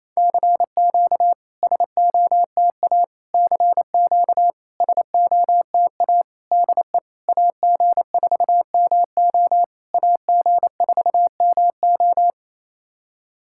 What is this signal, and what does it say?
Morse code, heard as a keyed tone, about 28 wpm, 700 Hz. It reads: CQ SOTA CQ SOTA DE AG4MO AG4MO
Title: SOTA QSO Practice 28wpm